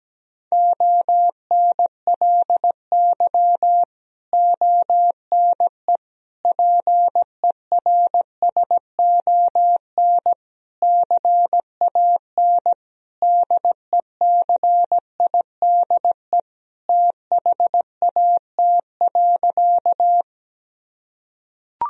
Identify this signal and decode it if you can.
Morse code, heard as a keyed tone, about 17 wpm, 700 Hz. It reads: ONLY ONE PERSON CAN DECIDE THAT.
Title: Sentences from Top 1000 Words - Review Part 3 - 17wpm